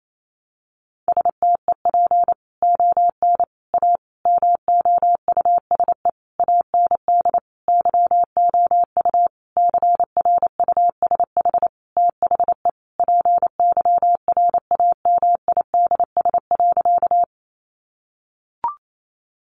Morse code, heard as a keyed tone, about 28 wpm, 700 Hz. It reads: STEP ON A MOUSE AND YOU CRUSH THE PYRAMIDS.
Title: A Sound Of Thunder 28wpm